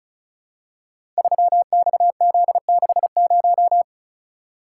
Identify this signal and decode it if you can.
Morse code, heard as a keyed tone, about 36 wpm, 700 Hz. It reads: 3XZ60
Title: Code Groups - 5 Letter:Numbers - 35wpm